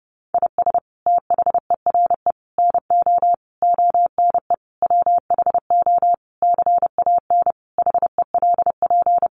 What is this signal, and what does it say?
Morse code, heard as a keyed tone, about 30 wpm, 700 Hz. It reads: IS THERE NO ONE WHO CAN HELP
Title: Sentences from Top 200 Words 30wpm